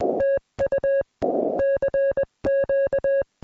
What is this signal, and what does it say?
Morse code, heard as a keyed tone, about 21 wpm, 550 Hz. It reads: TU CQ